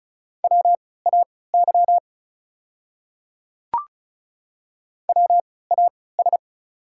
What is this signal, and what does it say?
Morse code, heard as a keyed tone, about 36 wpm, spaced as for 20 wpm, 690 Hz. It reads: WAY WAS
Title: 3x Character Spacing - Top 100 Words - 35wpm